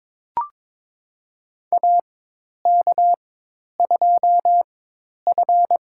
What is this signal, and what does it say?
Morse code, heard as a keyed tone, about 22 wpm, spaced as for 10 wpm, 700 Hz. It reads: AK2F
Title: USA Call Signs - 4 Characters - 4x Character Spacing - 22wpm